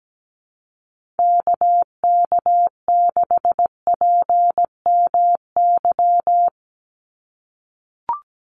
Morse code, heard as a keyed tone, about 17 wpm, 700 Hz. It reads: KK6PMY
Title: USA Call Signs 17wpm